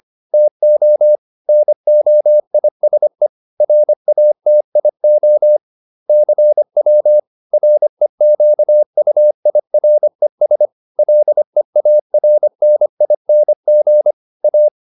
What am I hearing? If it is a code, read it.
Morse code, heard as a keyed tone, about 25 wpm, 600 Hz. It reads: TO NOISE RATIO CW REQUIRES LEARNING A